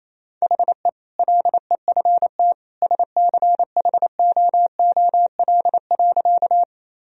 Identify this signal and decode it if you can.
Morse code, heard as a keyed tone, about 27 wpm, 700 Hz. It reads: HE LEFT SCHOOL.
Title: Sentences from Top 1000 Words - Review Part 1 - 28wpm